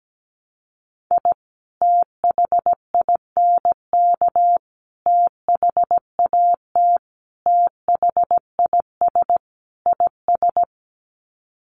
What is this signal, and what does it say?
Morse code, heard as a keyed tone, about 17 wpm, 700 Hz. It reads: I THINK THAT THIS IS
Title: Sets of 5 Words Encore2 17wpm